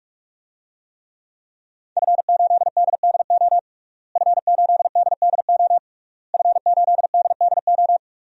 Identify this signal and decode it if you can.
Morse code, heard as a keyed tone, about 43 wpm, 700 Hz. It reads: F8DDO F8DDO F8DDO
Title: DXCC-100 Call Signs - 5 Characters - Repeated 3x - 45wpm